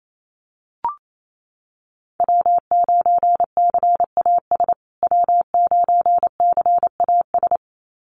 Morse code, heard as a keyed tone, about 28 wpm, 700 Hz. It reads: W9CAS W9CAS
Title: USA Call Signs Repeated 2x 28wpm